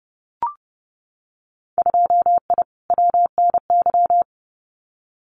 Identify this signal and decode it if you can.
Morse code, heard as a keyed tone, about 30 wpm, 700 Hz. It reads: 2I WNY
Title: Winter Field Day Exchanges Encore 30wpm